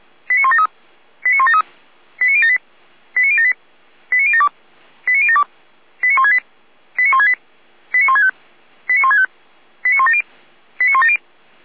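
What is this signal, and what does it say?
Five-tone selective call (ZVEI), heard as quick runs of five tones, tones of about 70 ms. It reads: 89173 89173 89078 89078 89072 89072 89168 89168 89165 89165 89170 89170
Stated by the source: ZVEI five-tone selective call received by SDR, from the feuerwehr-orlamuende.org fire brigade audio archive.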